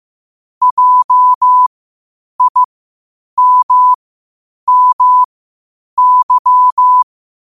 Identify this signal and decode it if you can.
Morse code, heard as a keyed tone, about 15 wpm, spaced as for 8 wpm, 1000 Hz. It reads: JIMMY
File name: qotdfiles / 8wpm